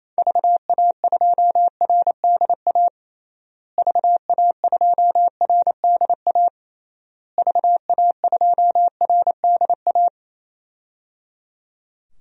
Morse code, heard as a keyed tone, about 28 wpm, 700 Hz. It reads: VA2RDA VA2RDA VA2RDA
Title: DXCC-100 Call Signs - 6 Characters - Repeated 3x - 28wpm